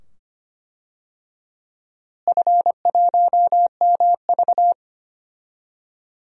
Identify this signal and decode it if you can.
Morse code, heard as a keyed tone, about 25 wpm, 700 Hz. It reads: F1MV
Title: DXCC-100 Call Signs - 4 Characters - Repeated 2x - 25wpm